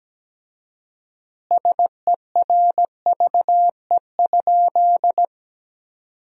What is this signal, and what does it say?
Morse code, heard as a keyed tone, about 17 wpm, 700 Hz. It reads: SERVE?
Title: 4x Word-Spacing - Sentences from Top 400 Words - 17wpm